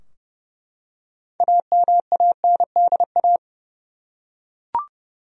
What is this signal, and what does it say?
Morse code, heard as a keyed tone, about 30 wpm, 700 Hz. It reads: AMANDA
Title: USA Common Names - 30wpm